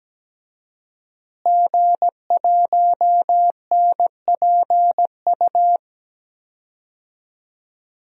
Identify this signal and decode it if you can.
Morse code, heard as a keyed tone, about 17 wpm, 700 Hz. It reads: G1NPU
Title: DXCC-100 Call Signs - 5 Characters - Repeated 2x - 17wpm